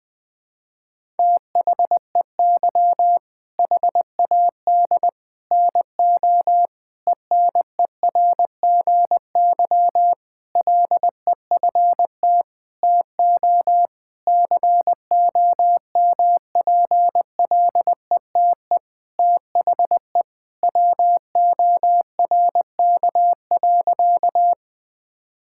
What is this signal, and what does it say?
Morse code, heard as a keyed tone, about 20 wpm, 700 Hz. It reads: THEY HAD NO ENERGY LEFT TO COMPLETE THE WORK.
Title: Sentences from Top 1000 Words - Review Part 6 - 20wpm